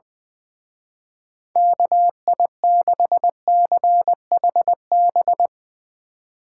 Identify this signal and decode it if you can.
Morse code, heard as a keyed tone, about 20 wpm, 700 Hz. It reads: KI6CHB
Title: Speed Racing - USA 2x3 Call Signs 20wpm